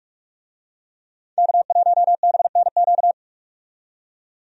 Morse code, heard as a keyed tone, about 46 wpm, 700 Hz. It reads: K1BNQ
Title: USA Call Signs - 5 Characters - 45wpm